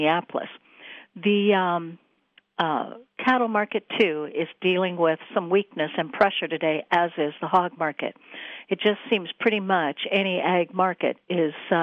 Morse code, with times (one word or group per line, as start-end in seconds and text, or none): none